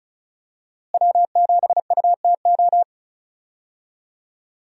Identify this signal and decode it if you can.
Morse code, heard as a keyed tone, about 36 wpm, 700 Hz. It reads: W7UTO
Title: USA Call Signs - 5 Characters - Repeated 3x - 35wpm